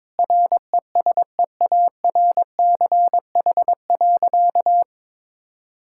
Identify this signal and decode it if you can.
Morse code, heard as a keyed tone, about 22 wpm, 700 Hz. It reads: RESEARCH.
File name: Sentences with 9-Word Phrases - 22wpm